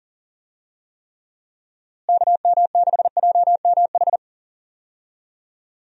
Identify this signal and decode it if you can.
Morse code, heard as a keyed tone, about 40 wpm, 700 Hz. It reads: KM6JMH
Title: USA Call Signs - 6 Characters - Repeated 3x - 40wpm